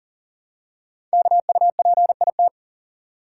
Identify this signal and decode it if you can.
Morse code, heard as a keyed tone, about 42 wpm, 700 Hz. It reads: KUPIT
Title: Code Groups - 5 Letter:Numbers - 40wpm